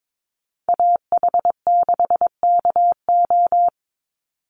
Morse code, heard as a keyed tone, about 22 wpm, 700 Hz. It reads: AH6KO
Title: Call Signs Repeated 3x - 22wpm